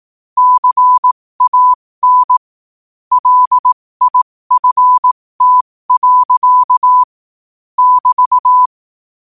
Morse code, heard as a keyed tone, about 18 wpm, spaced as for 15 wpm, 1000 Hz. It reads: CAN LIFT. <BT>